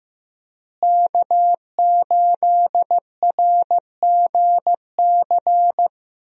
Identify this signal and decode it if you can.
Morse code, heard as a keyed tone, about 15 wpm, 700 Hz. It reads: K8RGC
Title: USA Call Signs - 5 Characters - Repeated 2x - 15wpm